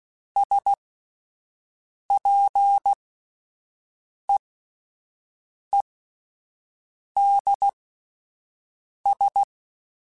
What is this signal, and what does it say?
Morse code, heard as a keyed tone, about 16 wpm, spaced as for 5 wpm, 800 Hz. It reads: SPEEDS
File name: Spaced Characters / Spaced Characters-mp3s / Weather4 24 sp @10